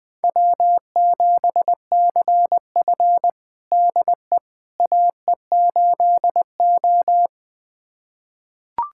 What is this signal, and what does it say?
Morse code, heard as a keyed tone, about 20 wpm, 700 Hz. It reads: W7CF DE AE8O
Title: USA Callsign DE Callsign - 4 Characters - 20wpm